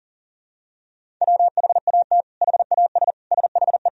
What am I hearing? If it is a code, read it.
Morse code, heard as a keyed tone, about 37 wpm, 700 Hz. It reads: WHAT HAS SHE
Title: Sentences from Top 100 Words and Abbreviations 40wpm